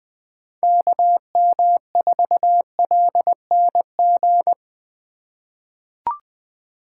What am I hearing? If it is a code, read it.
Morse code, heard as a keyed tone, about 21 wpm, 700 Hz. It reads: KM4LNG
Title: USA Call Signs - 6 Characters - 2x Character Spacing - 20wpm